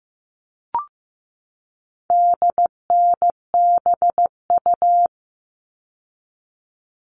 Morse code, heard as a keyed tone, about 15 wpm, 700 Hz. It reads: DNBU
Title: All Cut Numbers - 4-Digit 15wpm